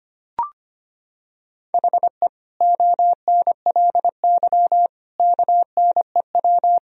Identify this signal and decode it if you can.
Morse code, heard as a keyed tone, about 25 wpm, 700 Hz. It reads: HE ONLY KNEW
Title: Sentences from Top 1000 Words - Review Part 2 - 25wpm